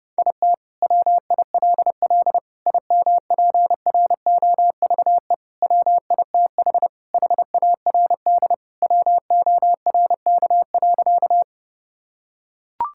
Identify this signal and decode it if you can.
Morse code, heard as a keyed tone, about 30 wpm, 700 Hz. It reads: IT WILL IMPROVE WITH HARD WORK.
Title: Sentences from Top 1000 Words - Review Part 5 - 30wpm